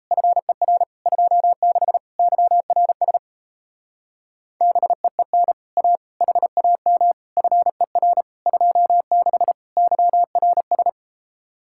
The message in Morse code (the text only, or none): FER 26 YRS BEEN A HAM FER 26 YRS